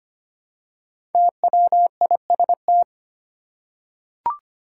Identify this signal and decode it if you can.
Morse code, heard as a keyed tone, about 25 wpm, 700 Hz. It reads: TWIST
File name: 5 Letter Words - First 3 Letters Given - 25wpm